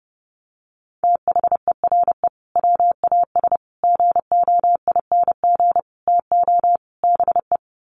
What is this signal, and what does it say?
Morse code, heard as a keyed tone, about 30 wpm, 700 Hz. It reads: THERE WAS GOING TO BE
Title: Sets of 5 Words Encore2 30wpm